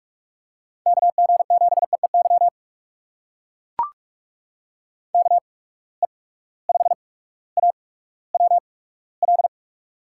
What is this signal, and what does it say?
Morse code, heard as a keyed tone, about 44 wpm, spaced as for 19 wpm, 700 Hz. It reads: KG7EEY KE5AWL
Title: USA Call Signs - 6 Characters - 8x Character Spacing - 45wpm